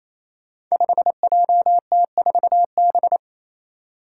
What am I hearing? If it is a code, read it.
Morse code, heard as a keyed tone, about 28 wpm, 700 Hz. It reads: 5JT4B
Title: Five Challenging Letter-Number Combinations 28wpm